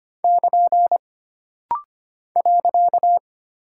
Morse code, heard as a keyed tone, about 25 wpm, 700 Hz. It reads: <KN> .
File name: Single Punctuation-Prosign - Rapid-Fire - 25wpm